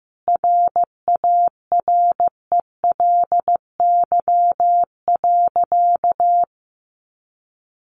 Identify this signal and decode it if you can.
Morse code, heard as a keyed tone, about 15 wpm, 700 Hz. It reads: RARELY.
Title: Sentences from Top 1400 Words 15wpm